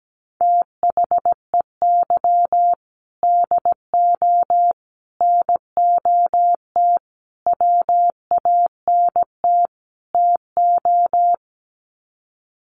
Morse code, heard as a keyed tone, about 17 wpm, 700 Hz. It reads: THEY DO NOT WANT TO
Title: Sets of 5 Words 17wpm